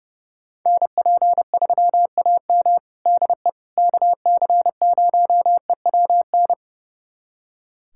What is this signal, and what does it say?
Morse code, heard as a keyed tone, about 30 wpm, 700 Hz. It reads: NP3AM DE KC0EWN
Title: USA Callsign DE Callsign 30wpm